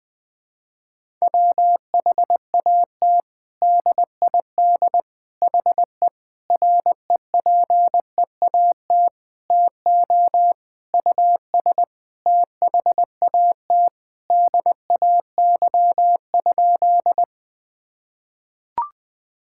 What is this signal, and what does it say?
Morse code, heard as a keyed tone, about 20 wpm, 700 Hz. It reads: WHAT DID HE REPEAT TO US THAT DAY?
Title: Sentences from Top 1600 Words 20wpm